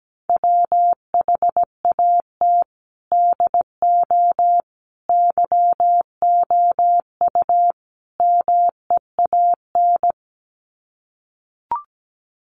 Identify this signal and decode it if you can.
Morse code, heard as a keyed tone, about 17 wpm, 700 Hz. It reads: WHAT DO YOU MEAN
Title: Sets of 4 Words 17wpm